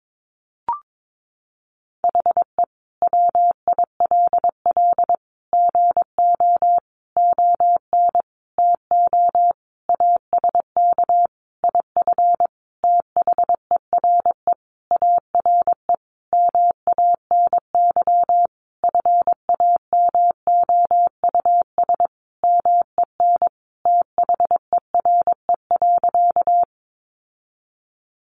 Morse code, heard as a keyed tone, about 22 wpm, 700 Hz. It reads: HE WILL GO ON TO ASK IF THERE ARE MANY FAMOUS MEN THERE.
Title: Sentences from Top 1500 Words 22wpm